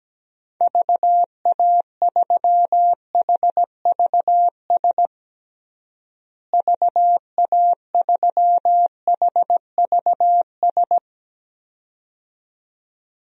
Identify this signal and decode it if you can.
Morse code, heard as a keyed tone, about 17 wpm, 700 Hz. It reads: VA3HVS VA3HVS
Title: DXCC-100 Call Signs - 6 Characters - Repeated 2x - 17wpm